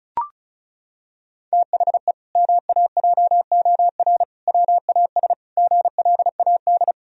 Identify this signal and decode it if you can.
Morse code, heard as a keyed tone, about 34 wpm, 700 Hz. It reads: THE MAJOR WAS GLAD
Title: Sentences from Top 1000 Words - Review Part 4 - 35wpm